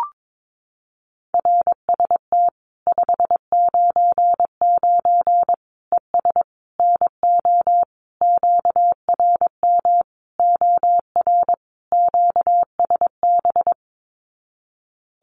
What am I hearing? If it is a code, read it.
Morse code, heard as a keyed tone, about 22 wpm, 700 Hz. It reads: RST 599 ES NO QRM OR QSB
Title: Common QSO Phrases 22wpm